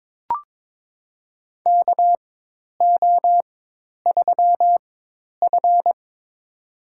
Morse code, heard as a keyed tone, about 23 wpm, spaced as for 10 wpm, 700 Hz. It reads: KO3F
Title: USA Call Signs - 4 Characters - 4x Character Spacing - 22wpm